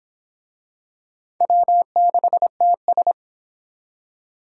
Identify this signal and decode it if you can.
Morse code, heard as a keyed tone, about 21 wpm, 700 Hz. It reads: W6TS
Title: Speed Racing - USA 1x2 Call Signs 17wpm